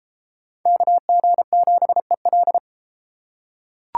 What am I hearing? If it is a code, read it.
Morse code, heard as a keyed tone, about 33 wpm, 700 Hz. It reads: KG7EL